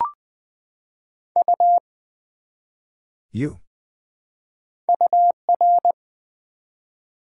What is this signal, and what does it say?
Morse code, heard as a keyed tone, about 21 wpm, 700 Hz. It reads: U UR
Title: Build A Word - 1 Letter At A Time - 4-Letter Words - 20wpm